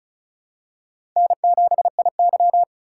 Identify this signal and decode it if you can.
Morse code, heard as a keyed tone, about 36 wpm, 700 Hz. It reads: N7IY
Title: USA Call Signs - 4 Characters - 3x Character Spacing - 35wpm